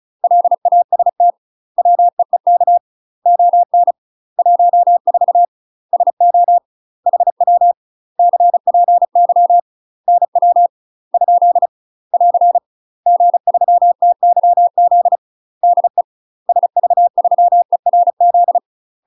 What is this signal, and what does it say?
Morse code, heard as a keyed tone, about 34 wpm, 700 Hz. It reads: LAST WEEK ON 14 SO HW CPY NW ? <AR> G3TYZ DE SV3ERZ
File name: UK MASTER QSOs TRAINING / 35WPM / UK MASTER QSOs 0004